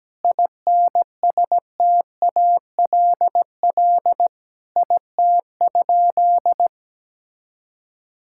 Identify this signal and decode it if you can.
Morse code, heard as a keyed tone, about 17 wpm, 700 Hz. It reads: INSTALL IT?
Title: Sentences from Top 1600 Words 17wpm